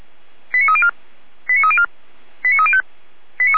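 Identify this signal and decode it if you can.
Five-tone selective call (ZVEI), heard as quick runs of five tones, tones of about 70 ms. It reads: 89304 89304 89305
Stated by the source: ZVEI five-tone selective call 89173 received by SDR, from the feuerwehr-orlamuende.org fire brigade server.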